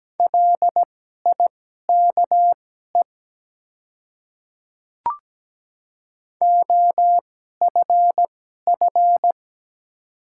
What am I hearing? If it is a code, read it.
Morse code, heard as a keyed tone, about 17 wpm, spaced as for 12 wpm, 700 Hz. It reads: LIKE OFF
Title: 2x Character Spacing - Top 200 Words - 17wpm